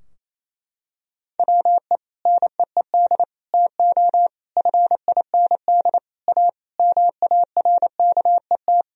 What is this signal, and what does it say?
Morse code, heard as a keyed tone, about 28 wpm, 700 Hz. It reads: WE NEED TO FIND A MARKET
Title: Sentences from Top 1000 Words - Review Part 3 - 28wpm